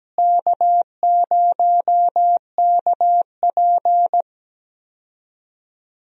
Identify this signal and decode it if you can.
Morse code, heard as a keyed tone, about 17 wpm, 700 Hz. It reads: K0KP
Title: USA Call Signs - 4 Characters - Repeated 3x - 17wpm